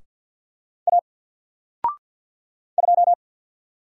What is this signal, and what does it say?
Morse code, heard as a keyed tone, about 48 wpm, 700 Hz. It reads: A 2
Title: Single Letter-Number 50wpm